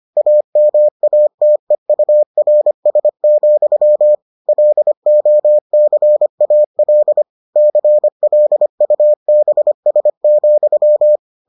Morse code, heard as a keyed tone, about 25 wpm, 600 Hz. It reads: AMATEURS, LOCAL CLUBS,